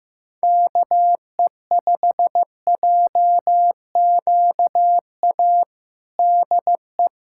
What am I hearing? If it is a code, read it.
Morse code, heard as a keyed tone, about 15 wpm, 700 Hz. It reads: KE5JQA DE
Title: USA Callsign DE Callsign 15wpm